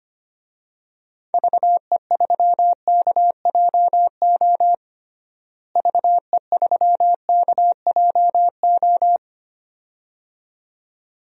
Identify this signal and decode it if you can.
Morse code, heard as a keyed tone, about 25 wpm, 700 Hz. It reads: VE3KJO VE3KJO
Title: DXCC-100 Call Signs - 6 Characters - Repeated 2x - 25wpm